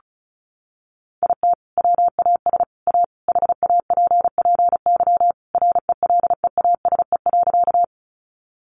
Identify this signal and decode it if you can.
Morse code, heard as a keyed tone, about 34 wpm, 700 Hz. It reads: IT WAS A HAPPY RELEASE.
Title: Sentences from Top 1100 Words 35wpm